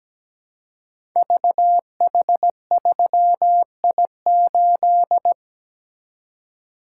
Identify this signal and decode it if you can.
Morse code, heard as a keyed tone, about 17 wpm, 700 Hz. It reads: VH3I8
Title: Code Groups - 5 Letter:Numbers - 17wpm